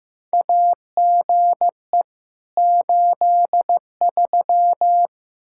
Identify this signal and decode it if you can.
Morse code, heard as a keyed tone, about 16 wpm, 700 Hz. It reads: AGE 83
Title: Practice QSO - Exchange 3 - More Info - 15wpm